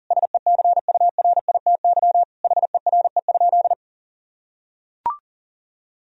Morse code, heard as a keyed tone, about 37 wpm, 700 Hz. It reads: SECURITY HERE?
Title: Sentences from Top 1000 Words - Review Part 4 - 40wpm